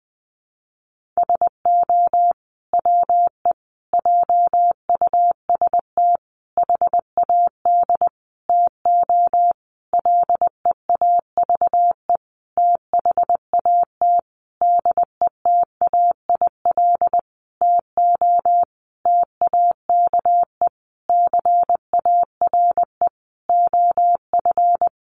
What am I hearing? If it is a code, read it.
Morse code, heard as a keyed tone, about 20 wpm, 700 Hz. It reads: SO WE JUST HAD TO LEAVE THAT DETAIL TO TAKE CARE OF
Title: Sentences from Top 1500 Words 20wpm